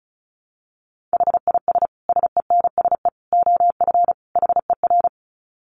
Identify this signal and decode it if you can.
Morse code, heard as a keyed tone, about 34 wpm, 700 Hz. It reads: HIS SENSE OF HER
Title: Dit Challenge 35wpm